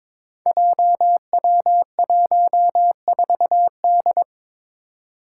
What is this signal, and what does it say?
Morse code, heard as a keyed tone, about 22 wpm, 700 Hz. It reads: JW14D
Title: Five Challenging Letter-Number Combinations 22wpm